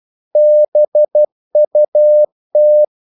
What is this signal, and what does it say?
Morse code, heard as a keyed tone, about 12 wpm, 600 Hz. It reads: BUT
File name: CW / Practice / If12